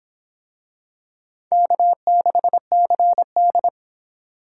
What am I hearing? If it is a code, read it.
Morse code, heard as a keyed tone, about 25 wpm, 700 Hz. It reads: K6CD